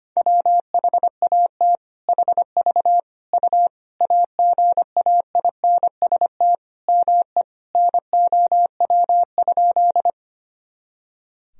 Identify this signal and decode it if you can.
Morse code, heard as a keyed tone, about 25 wpm, 700 Hz. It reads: WHAT HV U AGAINST ME NOW?
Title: Sentences from Top 400 Words and Abbreviations 25wpm